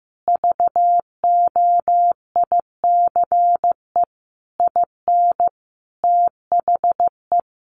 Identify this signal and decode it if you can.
Morse code, heard as a keyed tone, about 15 wpm, 700 Hz. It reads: VOICE IN THE
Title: Sentences from Top 500 Words 15wpm